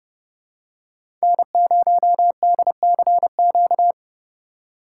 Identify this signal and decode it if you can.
Morse code, heard as a keyed tone, about 30 wpm, 700 Hz. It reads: N0DCQ
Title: USA Call Signs - 3x Character Spacing - 30wpm